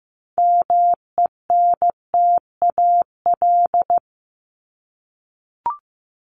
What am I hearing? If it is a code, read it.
Morse code, heard as a keyed tone, about 15 wpm, 700 Hz. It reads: MENTAL
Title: Top 501-750 Words 15wpm